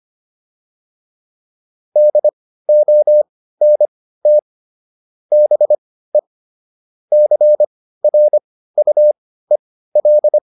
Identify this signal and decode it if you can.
Morse code, heard as a keyed tone, about 25 wpm, spaced as for 15 wpm, 600 Hz. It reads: DONT BE CRUEL